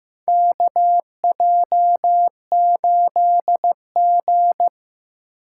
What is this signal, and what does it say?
Morse code, heard as a keyed tone, about 15 wpm, 700 Hz. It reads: KJ8G
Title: USA Call Signs - 4 Characters - 4x Character Spacing - 15wpm